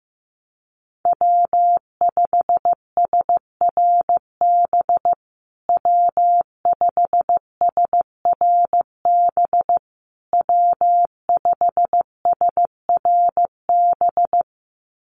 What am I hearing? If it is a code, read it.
Morse code, heard as a keyed tone, about 15 wpm, 700 Hz. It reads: W5SRB W5SRB W5SRB
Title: USA Call Signs Repeated 3x 15wpm